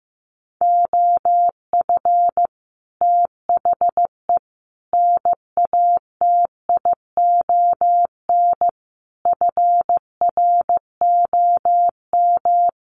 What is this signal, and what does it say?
Morse code, heard as a keyed tone, about 15 wpm, 700 Hz. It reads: OF THE NATION FROM
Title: Sets of 5 Words 15wpm